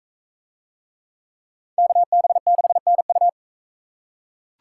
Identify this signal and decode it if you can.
Morse code, heard as a keyed tone, about 43 wpm, 700 Hz. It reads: KB6NU